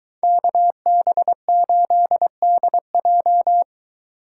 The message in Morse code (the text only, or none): KB8DJ